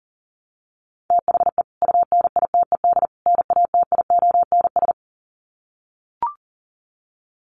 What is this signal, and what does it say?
Morse code, heard as a keyed tone, about 40 wpm, 700 Hz. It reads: THE UNITED NATIONS
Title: Sets of 3 Words 40wpm